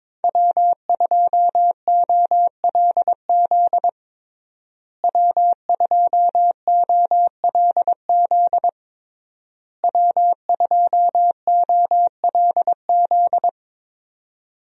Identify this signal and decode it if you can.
Morse code, heard as a keyed tone, about 22 wpm, 700 Hz. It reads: W2OLZ W2OLZ W2OLZ
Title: USA Call Signs - 5 Characters - Repeated 3x - 22wpm